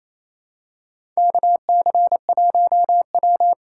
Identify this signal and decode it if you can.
Morse code, heard as a keyed tone, about 28 wpm, 700 Hz. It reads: KC1W
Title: USA Call Signs - 4 Characters - 8x Character Spacing - 28wpm